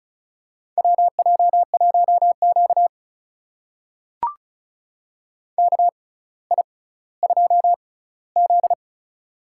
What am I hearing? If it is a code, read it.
Morse code, heard as a keyed tone, about 36 wpm, spaced as for 18 wpm, 700 Hz. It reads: WJ1Q KI2Z
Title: USA Call Signs - 4 Characters - 6x Character Spacing - 35wpm